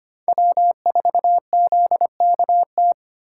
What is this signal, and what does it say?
Morse code, heard as a keyed tone, about 25 wpm, 700 Hz. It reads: W4ZKT
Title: USA Call Signs - 5 Characters - 3x Character Spacing - 25wpm